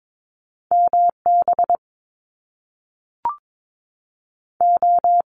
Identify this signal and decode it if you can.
Morse code, heard as a keyed tone, about 22 wpm, 700 Hz. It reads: MB O
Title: Top Two Letter Combinations in Words 22wpm